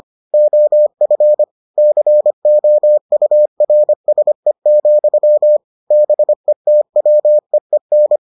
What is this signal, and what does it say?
Morse code, heard as a keyed tone, about 25 wpm, 600 Hz. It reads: OF COURSE, BETWEEN